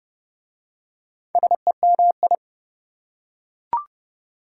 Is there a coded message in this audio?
Yes: Morse code from 1.4 to 2.4 s, heard as a keyed tone, about 28 wpm, 700 Hz. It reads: SEMI